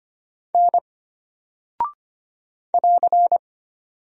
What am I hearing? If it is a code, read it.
Morse code, heard as a keyed tone, about 25 wpm, 700 Hz. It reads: N <AR>
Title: Single Character (Letter:Number:Punctuation:Prosign) Alt 25wpm